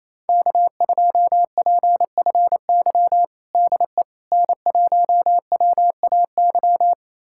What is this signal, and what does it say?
Morse code, heard as a keyed tone, about 28 wpm, 700 Hz. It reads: K2PFY DE N1WAY
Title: USA Callsign DE Callsign 28wpm